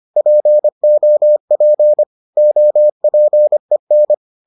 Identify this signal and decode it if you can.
Morse code, heard as a keyed tone, about 25 wpm, 600 Hz. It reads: POP OPEN